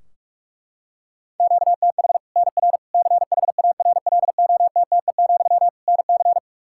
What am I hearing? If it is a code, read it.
Morse code, heard as a keyed tone, about 46 wpm, 700 Hz. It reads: QTH NR CHARLOTTE, NC
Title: Practice QSO - Exchange 1 - Beginning - 45wpm